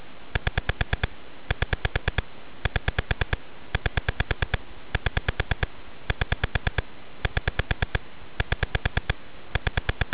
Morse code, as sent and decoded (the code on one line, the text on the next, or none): none